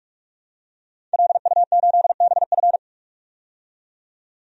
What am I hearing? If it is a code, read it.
Morse code, heard as a keyed tone, about 46 wpm, 700 Hz. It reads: LU8BF
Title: DXCC-100 Call Signs - 5 Characters - 45wpm